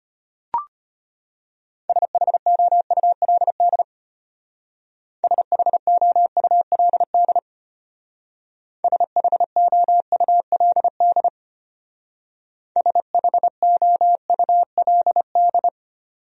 Morse code, heard as a keyed tone, about 30 wpm, 700 Hz. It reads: SHOULD SHOULD SHOULD SHOULD
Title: Speed-Racing Top 300 Words 25wpm